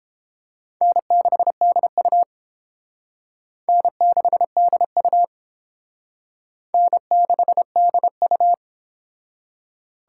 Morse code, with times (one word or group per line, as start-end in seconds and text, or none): 0.8-2.2 N6DU
3.7-5.2 N6DU
6.7-8.5 N6DU